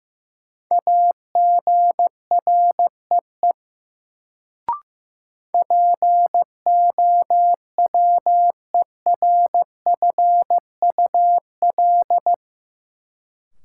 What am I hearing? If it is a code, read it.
Morse code, heard as a keyed tone, about 15 wpm, 700 Hz. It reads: AGREE POWERFUL
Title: Sentences from Top 1000 Words - Review Part 6 - New Word Drill - 15wpm